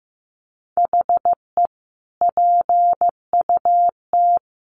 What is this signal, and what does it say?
Morse code, heard as a keyed tone, about 15 wpm, 700 Hz. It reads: HE PUT
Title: Sentences from Top 1400 Words 15wpm